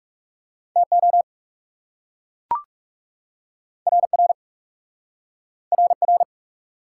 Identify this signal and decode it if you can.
Morse code, heard as a keyed tone, about 37 wpm, 700 Hz. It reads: TO RR RR
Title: Speed-Racing Top 180 QSO Elements 30wpm